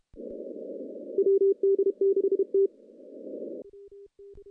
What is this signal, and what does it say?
Morse code, heard as a keyed tone, about 27 wpm, 410 Hz. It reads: WD6T
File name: contests / CW-SS-k5tr- -0700